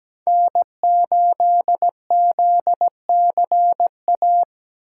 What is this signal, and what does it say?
Morse code, heard as a keyed tone, about 17 wpm, 700 Hz. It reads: N8ZCA
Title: USA Call Signs - 5 Characters - 4x Character Spacing - 17wpm